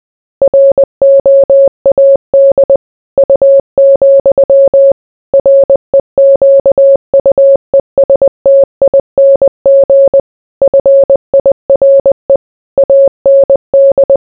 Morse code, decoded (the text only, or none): ROAD U, REQUESTING FIRE AND